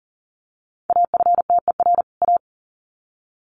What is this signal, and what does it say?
Morse code, heard as a keyed tone, about 40 wpm, 700 Hz. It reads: AFTER A
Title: Sets of 2 Words 40wpm